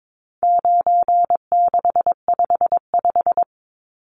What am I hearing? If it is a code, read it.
Morse code, heard as a keyed tone, about 22 wpm, 700 Hz. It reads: 9655
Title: Numbers - 4-Digit 22wpm